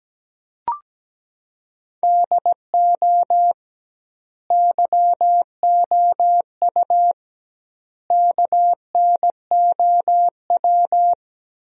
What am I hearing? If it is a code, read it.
Morse code, heard as a keyed tone, about 17 wpm, 700 Hz. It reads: DO YOU KNOW
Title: 2x Word-Spacing - Sentences from Top 100 Words - 17wpm